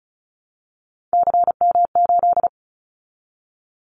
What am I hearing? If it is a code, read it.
Morse code, heard as a keyed tone, about 36 wpm, 700 Hz. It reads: CM8
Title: Callsign Prefixes for Top 100 DXCC - 35wpm